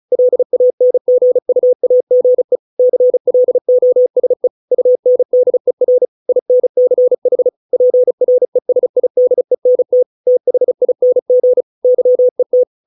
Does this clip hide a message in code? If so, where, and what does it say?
Yes: Morse code from 0.1 to 12.6 s, heard as a keyed tone, about 34 wpm, 500 Hz. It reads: LANGUAGE CLOSE UNDER INCH PRESIDENT THING YET